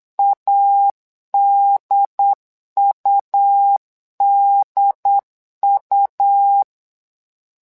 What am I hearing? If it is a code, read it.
Morse code, heard as a keyed tone, about 9 wpm, 800 Hz. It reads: ADUDU
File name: LEZIONI / 2 UHDN / 2 UHDN 8wpm2232